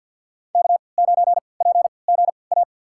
Morse code, heard as a keyed tone, about 52 wpm, spaced as for 29 wpm, 690 Hz. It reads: K9PGA
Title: USA Call Signs - 5 Characters - 3x Character Spacing - 50wpm